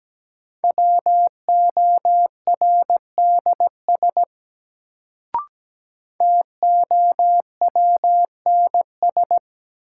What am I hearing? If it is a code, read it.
Morse code, heard as a keyed tone, about 17 wpm, 700 Hz. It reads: WORDS TOWNS
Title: Sentences from Top 1000 Words - Review Part 6 - New Word Drill - 17wpm